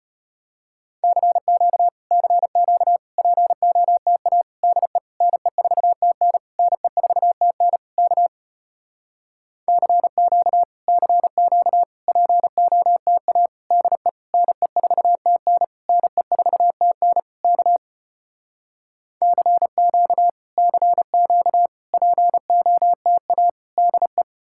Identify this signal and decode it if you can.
Morse code, heard as a keyed tone, about 33 wpm, 700 Hz. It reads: CQ CQ POTA DE NE4TN NE4TN K CQ CQ POTA DE NE4TN NE4TN K CQ CQ POTA DE